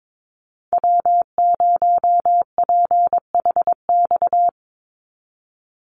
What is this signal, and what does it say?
Morse code, heard as a keyed tone, about 22 wpm, 700 Hz. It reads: W0PHX
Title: Call Signs Repeated 3x - 22wpm